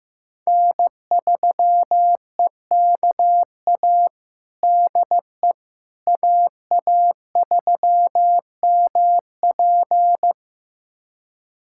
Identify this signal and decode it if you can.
Morse code, heard as a keyed tone, about 16 wpm, 700 Hz. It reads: N3EKA DE AA3MP
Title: USA Callsign DE Callsign - 5 Characters - 15wpm